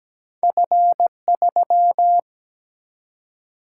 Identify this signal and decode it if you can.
Morse code, heard as a keyed tone, about 17 wpm, 700 Hz. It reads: F3
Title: Single-Letter Single-Number 17wpm